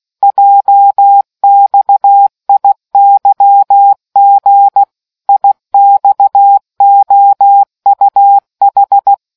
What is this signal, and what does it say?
Morse code, heard as a keyed tone, about 16 wpm, 800 Hz. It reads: JXIYG IXOUH